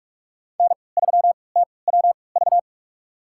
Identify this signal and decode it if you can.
Morse code, heard as a keyed tone, about 47 wpm, spaced as for 26 wpm, 700 Hz. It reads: N3TWV
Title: USA Call Signs - 3x Character Spacing - 45wpm